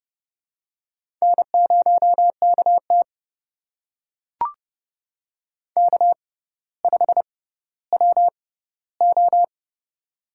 Morse code, heard as a keyed tone, about 30 wpm, spaced as for 15 wpm, 700 Hz. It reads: N0KT K5WO
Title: USA Call Signs - 4 Characters - 6x Character Spacing - 30wpm